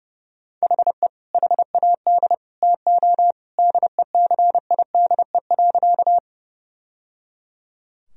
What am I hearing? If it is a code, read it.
Morse code, heard as a keyed tone, about 30 wpm, 700 Hz. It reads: HE HAD TO DECIDE.
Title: Sentences from Top 1000 Words - Review Part 3 - 30wpm